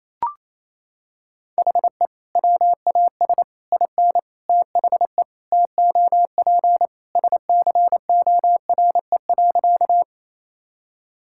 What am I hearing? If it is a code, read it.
Morse code, heard as a keyed tone, about 27 wpm, 700 Hz. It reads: HE WAS IN THE TOP SCORE.
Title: Sentences from Top 1600 Words 28wpm